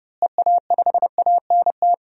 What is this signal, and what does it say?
Morse code, heard as a keyed tone, about 30 wpm, 700 Hz. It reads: EA5ANT
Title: DXCC-100 Call Signs - 6 Characters - Repeated 2x - 30wpm